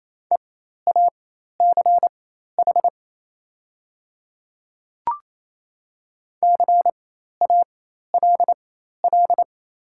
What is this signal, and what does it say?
Morse code, heard as a keyed tone, about 28 wpm, spaced as for 13 wpm, 700 Hz. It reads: EACH CALL
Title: 4x Character Spacing - Top 200 Words - 28wpm